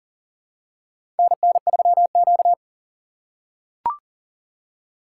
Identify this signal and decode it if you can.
Morse code, heard as a keyed tone, about 37 wpm, 700 Hz. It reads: NN3Q
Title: USA Call Signs - 4 Characters - 2x Character Spacing - 40wpm